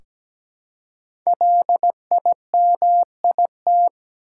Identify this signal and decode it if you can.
Morse code, heard as a keyed tone, about 17 wpm, 700 Hz. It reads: LIMIT
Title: Five Letter Words - 17wpm